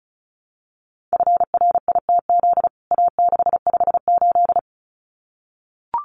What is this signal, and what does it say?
Morse code, heard as a keyed tone, about 34 wpm, 700 Hz. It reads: FRITZ A658
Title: CWT Exchange 35wpm